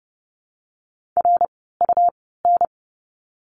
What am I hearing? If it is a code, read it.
Morse code, heard as a keyed tone, about 30 wpm, spaced as for 17 wpm, 700 Hz. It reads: RUN
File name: course / 30:10wpm / 115 Lesson 19f - Letter w - Words 30wpm